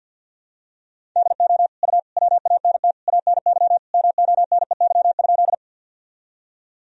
Morse code, heard as a keyed tone, about 47 wpm, 700 Hz. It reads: DO U WANT ANY MONEY?
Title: Sentences from Top 400 Words and Abbreviations 50wpm